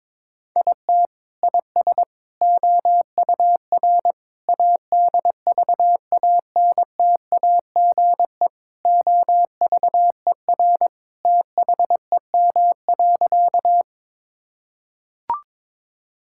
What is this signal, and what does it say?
Morse code, heard as a keyed tone, about 22 wpm, 700 Hz. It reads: IT IS OUR ADVANTAGE OVER THEM.
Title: Sentences from Top 1000 Words - Review Part 6 - 22wpm